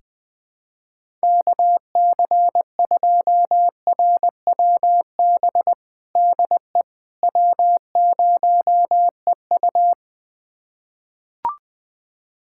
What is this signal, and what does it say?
Morse code, heard as a keyed tone, about 20 wpm, 700 Hz. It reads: KC2RWB DE W0EU
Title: USA Callsign DE Callsign 20wpm